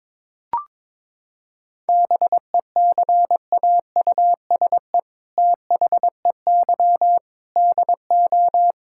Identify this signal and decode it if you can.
Morse code, heard as a keyed tone, about 22 wpm, 700 Hz. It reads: BECAUSE THEY DO
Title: Sentences from Top 1000 Words - Review Part 5 - 22wpm